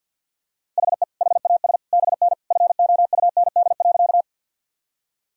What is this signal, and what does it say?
Morse code, heard as a keyed tone, about 50 wpm, 700 Hz. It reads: HE HAS BN FOUND.
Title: Sentences from Top 300 Words and Abbreviations 50wpm